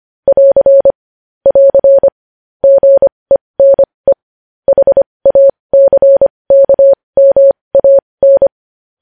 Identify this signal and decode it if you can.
Morse code, heard as a keyed tone, about 25 wpm, spaced as for 20 wpm, 550 Hz. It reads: <AR> <AR> GENE HACKMAN
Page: News Headlines in Morse Code at 20 WPM